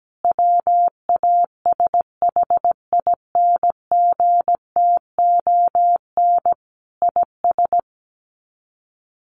Morse code, heard as a keyed tone, about 17 wpm, 700 Hz. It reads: WASHINGTON IS
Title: Sets of 5 Words Encore2 17wpm